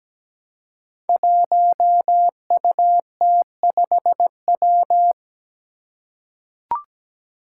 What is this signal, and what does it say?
Morse code, heard as a keyed tone, about 17 wpm, 700 Hz. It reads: 1UT5W
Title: Five Challenging Letter-Number Combinations 17wpm